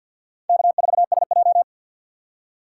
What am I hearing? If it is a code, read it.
Morse code, heard as a keyed tone, about 61 wpm, 700 Hz. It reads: K4SJ
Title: Speed Racing - USA 1x2 Call Signs 50wpm